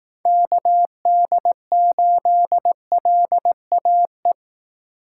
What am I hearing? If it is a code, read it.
Morse code, heard as a keyed tone, about 18 wpm, 700 Hz. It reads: KD8LAE
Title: Speed Racing - USA 2x3 Call Signs 15wpm